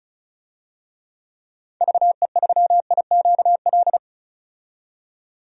Morse code, heard as a keyed tone, about 34 wpm, 700 Hz. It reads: VE3IQL
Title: DXCC-100 Call Signs - 6 Characters - 35wpm